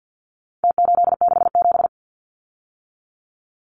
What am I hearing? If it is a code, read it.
Morse code, heard as a keyed tone, about 50 wpm, 700 Hz. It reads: T867
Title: Numbers - Contest Style - 4-Digit 50wpm